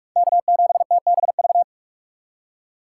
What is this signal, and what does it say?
Morse code, heard as a keyed tone, about 46 wpm, 700 Hz. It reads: K7TBV
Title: USA Call Signs - 5 Characters - 3x Character Spacing - 45wpm